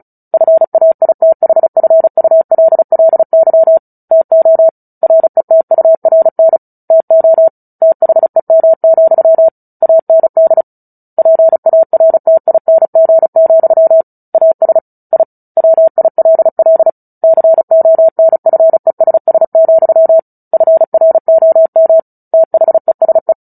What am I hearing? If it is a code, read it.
Morse code, heard as a keyed tone, about 34 wpm, 650 Hz. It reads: FAITHFULLY TO RETURN TO THEM, AND PARTING, AS I WILL CONFESS, FROM THESE
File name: Book Two / 09-Chapter 9 - 35 WPM